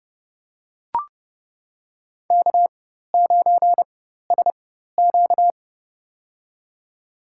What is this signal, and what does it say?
Morse code, heard as a keyed tone, about 30 wpm, spaced as for 14 wpm, 700 Hz. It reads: K9SQ
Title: USA Call Signs - 4 Characters - 4x Character Spacing - 30wpm